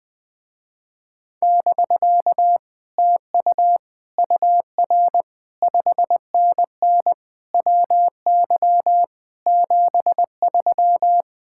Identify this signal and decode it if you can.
Morse code, heard as a keyed tone, about 20 wpm, 700 Hz. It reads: <BK> TU UR 5NN WY 73